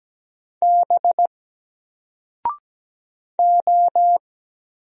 Morse code, heard as a keyed tone, about 17 wpm, 700 Hz. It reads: B O
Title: Single Letter-Number 17wpm